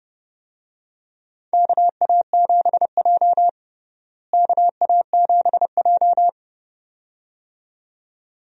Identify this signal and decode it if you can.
Morse code, heard as a keyed tone, about 30 wpm, 700 Hz. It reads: KA7J KA7J
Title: USA Call Signs - 4 Characters - Repeated 2x - 30wpm